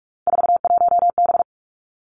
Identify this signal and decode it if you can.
Morse code, heard as a keyed tone, about 46 wpm, 700 Hz. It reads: 41B
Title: Three Challenging Letter-Number Combinations 45wpm